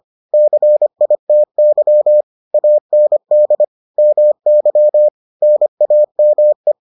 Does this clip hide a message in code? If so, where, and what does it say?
Yes: Morse code from 0.3 to 6.7 s, heard as a keyed tone, about 25 wpm, 600 Hz. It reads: CITY AND MY NAME